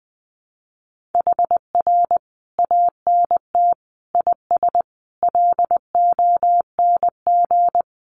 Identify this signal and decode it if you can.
Morse code, heard as a keyed tone, about 20 wpm, 700 Hz. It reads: HR ANT IS LONG
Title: Common QSO Phrases 20wpm